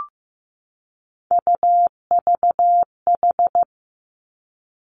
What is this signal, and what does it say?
Morse code, heard as a keyed tone, about 15 wpm, 700 Hz. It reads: UVH
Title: Three Challenging Letter-Number Combinations 15wpm